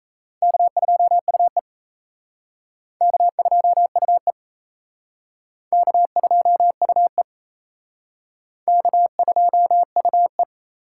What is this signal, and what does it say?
Morse code, heard as a keyed tone, about 34 wpm, 700 Hz. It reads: K2UE K2UE K2UE K2UE
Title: Speed Racing - USA 1x2 Call Signs 28wpm